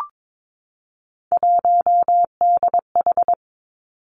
Morse code, heard as a keyed tone, about 22 wpm, 700 Hz. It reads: 1DH
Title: Three Challenging Letter-Number Combinations 22wpm